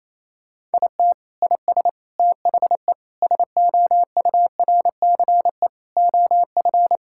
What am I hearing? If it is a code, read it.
Morse code, heard as a keyed tone, about 28 wpm, 700 Hz. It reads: IT IS THE SOURCE OF
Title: Sentences from Top 1000 Words - Review Part 5 - 28wpm